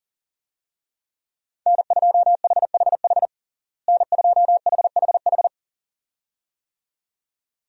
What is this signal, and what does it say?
Morse code, heard as a keyed tone, about 40 wpm, 700 Hz. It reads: N2HHH N2HHH
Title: USA Call Signs - 5 Characters - Repeated 2x - 40wpm